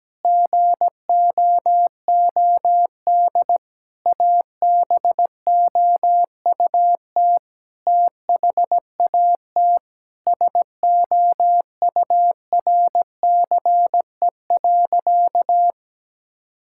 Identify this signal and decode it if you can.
Morse code, heard as a keyed tone, about 17 wpm, 700 Hz. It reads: GOOD ABOUT THAT SOURCE.
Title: Sentences from Top 1000 Words - Review Part 5 - 17wpm